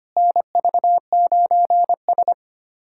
Morse code, heard as a keyed tone, about 25 wpm, 700 Hz. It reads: NV9S
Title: USA Callsign DE Callsign - 4 Characters - 25wpm